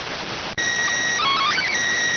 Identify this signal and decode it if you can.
SSTV (slow-scan television), PD 290.